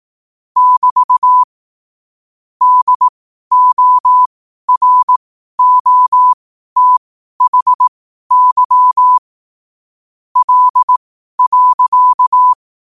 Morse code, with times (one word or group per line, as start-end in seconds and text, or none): 0.6-1.4 <BT>
2.6-9.2 DOROTHY
10.4-12.5 L.